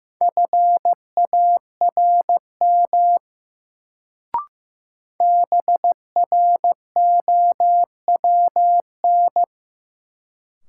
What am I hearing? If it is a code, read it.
Morse code, heard as a keyed tone, about 15 wpm, 700 Hz. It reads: FARM BROWN
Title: Sentences from Top 1200 Words - New Word Drill Alt - 15wpm